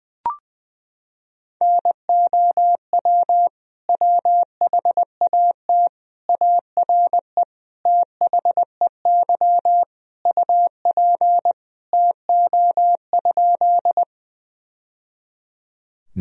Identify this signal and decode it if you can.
Morse code, heard as a keyed tone, about 20 wpm, 700 Hz. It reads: NOW WHAT ARE THEY UP TO?
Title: Sentences from Top 1000 Words - Review Part 1 - 20wpm